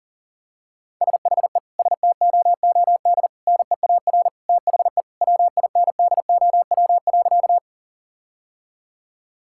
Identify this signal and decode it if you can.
Morse code, heard as a keyed tone, about 40 wpm, 700 Hz. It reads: SHE STOOD NEAR THE WINDOW.
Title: Sentences from Top 1600 Words 40wpm